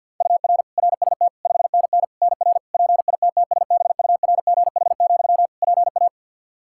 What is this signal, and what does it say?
Morse code, heard as a keyed tone, about 47 wpm, 700 Hz. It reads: UR RST 5NN NR PITTSBURGH, PA
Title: Practice QSO - Exchange 1 - Beginning - 50wpm